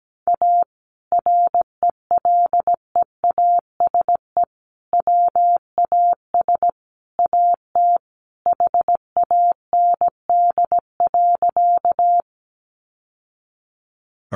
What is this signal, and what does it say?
Morse code, heard as a keyed tone, about 17 wpm, 700 Hz. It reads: A RELEASE WAS AT HAND.
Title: Sentences from Top 1100 Words 17wpm